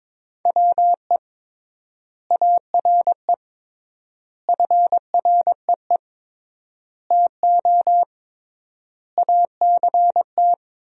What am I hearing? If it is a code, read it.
Morse code, heard as a keyed tone, about 22 wpm, 700 Hz. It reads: WE ARE FREE TO ACT
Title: 3x Word-Spacing - Sentences from Top 500 Words - 22wpm